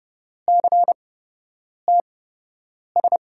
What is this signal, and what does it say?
Morse code, heard as a keyed tone, about 30 wpm, spaced as for 8 wpm, 700 Hz. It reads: CTS
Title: USA Call Signs - 6 Characters - 8x Character Spacing - 30wpm